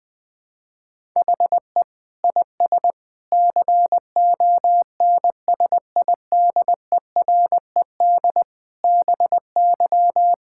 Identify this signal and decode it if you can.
Morse code, heard as a keyed tone, about 20 wpm, 700 Hz. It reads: HE IS CONSIDERED BY
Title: Sentences with 7-Word Phrases - 20wpm